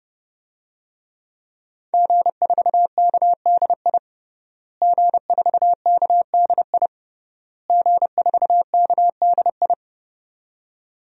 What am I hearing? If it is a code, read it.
Morse code, heard as a keyed tone, about 30 wpm, 700 Hz. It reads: G4KDI G4KDI G4KDI
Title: DXCC-100 Call Signs - 5 Characters - Repeated 3x - 30wpm